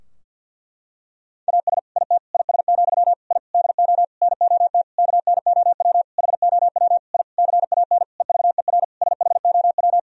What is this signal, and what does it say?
Morse code, heard as a keyed tone, about 50 wpm, 700 Hz. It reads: AS IT IS, I DO NOT KNOW HOW I CAN EVER SHOW
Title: Sentences from Top 1000 Words - Review Part 2 - 50wpm